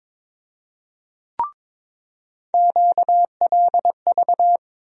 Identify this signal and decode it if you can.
Morse code, heard as a keyed tone, about 22 wpm, 700 Hz. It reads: QLV
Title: USA Call Sign Postfixes - 3 Letters 22wpm